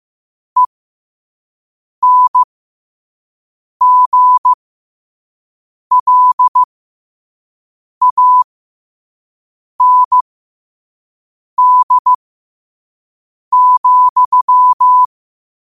Morse code, heard as a keyed tone, about 15 wpm, spaced as for 5 wpm, 1000 Hz. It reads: ENGLAND,
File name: qotdfiles / 5wpm